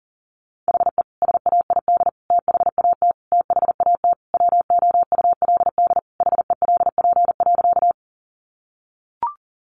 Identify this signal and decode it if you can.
Morse code, heard as a keyed tone, about 40 wpm, 700 Hz. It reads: HE SAID THAT THAT WOULD HELP.
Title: Sentences from Top 200 Words 40wpm